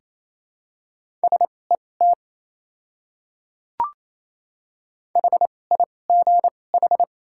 Morse code, heard as a keyed tone, about 28 wpm, spaced as for 20 wpm, 700 Hz. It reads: SET HIGH
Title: 2x Character Spacing - Top 200 Words - 28wpm